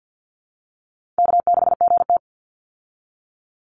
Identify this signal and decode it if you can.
Morse code, heard as a keyed tone, about 50 wpm, 700 Hz. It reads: K6GT
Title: Easy Call Signs Repeated 2x - 50wpm